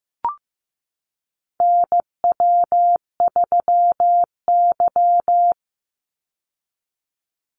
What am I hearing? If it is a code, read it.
Morse code, heard as a keyed tone, about 15 wpm, 700 Hz. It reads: NW3Y
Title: Call Signs Repeated 2x - 15wpm